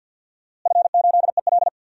Morse code, heard as a keyed tone, about 53 wpm, 700 Hz. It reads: F8EL
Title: DXCC-100 Call Signs - 4 Characters - Repeated 3x - 50wpm